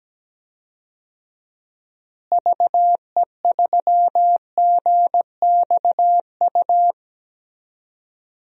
Morse code, heard as a keyed tone, about 17 wpm, 700 Hz. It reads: VE3GXU
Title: DXCC-100 Call Signs - 6 Characters - Repeated 2x - 17wpm